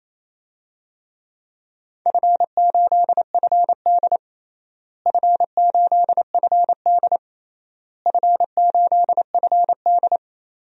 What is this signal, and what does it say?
Morse code, heard as a keyed tone, about 28 wpm, 700 Hz. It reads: F8FD F8FD F8FD
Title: DXCC-100 Call Signs - 4 Characters - Repeated 3x - 28wpm